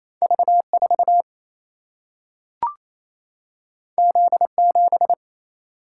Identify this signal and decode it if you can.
Morse code, heard as a keyed tone, about 28 wpm, 700 Hz. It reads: V 4 Z 7
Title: Single-Letter Single-Number 28wpm